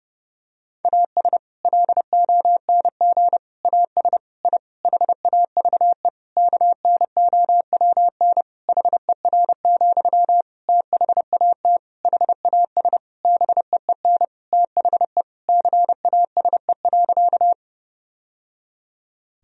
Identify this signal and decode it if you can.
Morse code, heard as a keyed tone, about 30 wpm, 700 Hz. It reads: AS LONG AS I HAVE KNOWN HER, THAT HAS BEEN THE CASE.
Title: Sentences from Top 1000 Words - Review Part 3 - 30wpm